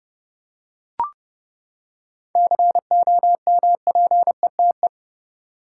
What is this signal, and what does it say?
Morse code, heard as a keyed tone, about 30 wpm, 700 Hz. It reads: COMPETE
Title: Seven Letter Words - 30wpm